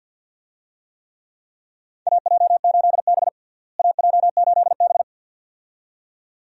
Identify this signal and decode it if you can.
Morse code, heard as a keyed tone, about 52 wpm, 700 Hz. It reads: AJ8B AJ8B
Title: USA Call Signs - 4 Characters - Repeated 2x - 50wpm